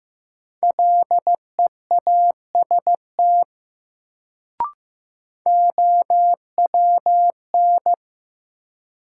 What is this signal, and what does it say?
Morse code, heard as a keyed tone, about 15 wpm, 700 Hz. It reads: LEAST OWN
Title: Sets of 2 Words - New Word Drill - 15wpm